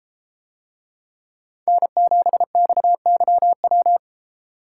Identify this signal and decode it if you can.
Morse code, heard as a keyed tone, about 32 wpm, 700 Hz. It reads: N7XYW